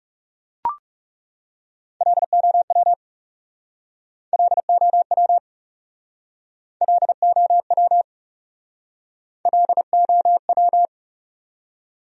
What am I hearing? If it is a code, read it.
Morse code, heard as a keyed tone, about 37 wpm, 700 Hz. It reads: LOW LOW LOW LOW
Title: Speed-Racing Top 200 Words 30wpm